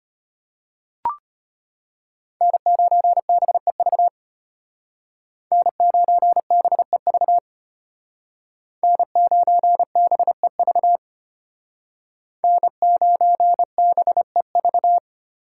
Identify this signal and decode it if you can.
Morse code, heard as a keyed tone, about 33 wpm, 700 Hz. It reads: N9BEV N9BEV N9BEV N9BEV